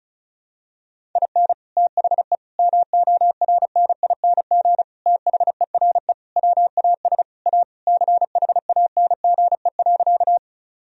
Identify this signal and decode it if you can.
Morse code, heard as a keyed tone, about 36 wpm, 700 Hz. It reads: IN THE MORNING THERE WAS A CHANGE.
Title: Sentences from Top 1000 Words - Review Part 1 - 35wpm